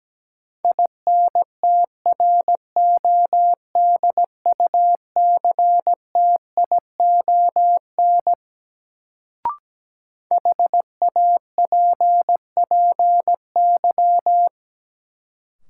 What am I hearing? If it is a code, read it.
Morse code, heard as a keyed tone, about 17 wpm, 700 Hz. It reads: INTRODUCTION HAPPY
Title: Sentences from Top 1000 Words - New Word Drill - 17wpm